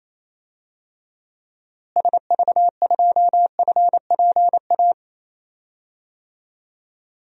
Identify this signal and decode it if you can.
Morse code, heard as a keyed tone, about 28 wpm, 700 Hz. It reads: SV2FPA
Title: DXCC-100 Call Signs - 6 Characters - Repeated 3x - 28wpm